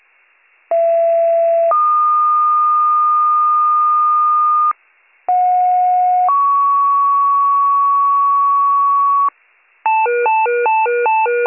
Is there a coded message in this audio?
Yes: two-tone page from 0.7 to 9.3 s, heard as a one-second tone, then a three-second tone, about 670 then 1220 Hz, about 730 then 1120 Hz.